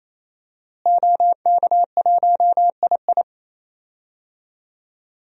Morse code, heard as a keyed tone, about 27 wpm, 700 Hz. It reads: OK1II
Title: DXCC-100 Call Signs - 5 Characters - Repeated 3x - 28wpm